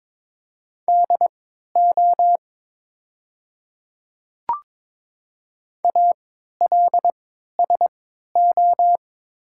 Morse code, heard as a keyed tone, about 21 wpm, spaced as for 12 wpm, 700 Hz. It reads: DO ALSO
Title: 3x Character Spacing - Top 200 Words - 22wpm